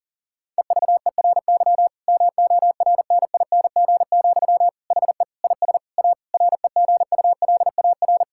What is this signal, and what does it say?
Morse code, heard as a keyed tone, about 40 wpm, 700 Hz. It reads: EVERY MORNING, HE IS A REGULAR
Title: Sentences from Top 1000 Words - Review Part 6 - 40wpm